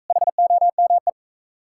Morse code, heard as a keyed tone, about 45 wpm, 700 Hz. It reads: HOME